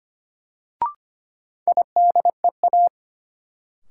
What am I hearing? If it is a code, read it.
Morse code, heard as a keyed tone, about 25 wpm, 700 Hz. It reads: IDEA
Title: Sentences from Top 300 Words - New Word Drill - 25wpm